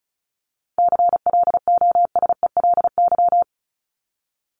Morse code, heard as a keyed tone, about 34 wpm, 700 Hz. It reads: CLOSELY
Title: Top 1251-1500 Words 35wpm